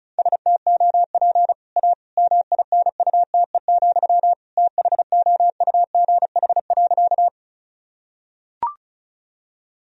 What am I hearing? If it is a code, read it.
Morse code, heard as a keyed tone, about 36 wpm, 700 Hz. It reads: STOP A MINUTE, THOUGH.
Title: Sentences from Top 1000 Words - Review Part 2 - 35wpm